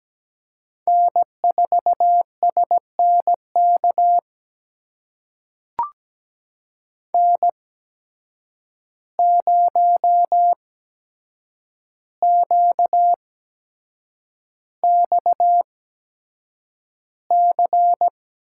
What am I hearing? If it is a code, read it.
Morse code, heard as a keyed tone, about 17 wpm, spaced as for 7 wpm, 700 Hz. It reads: N4SNK N0QXC
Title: USA Call Signs - 5 Characters - 8x Character Spacing - 17wpm